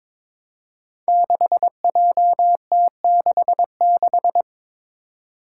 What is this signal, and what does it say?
Morse code, heard as a keyed tone, about 23 wpm, 700 Hz. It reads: 6JT66
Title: Five Challenging Letter-Number Combinations 22wpm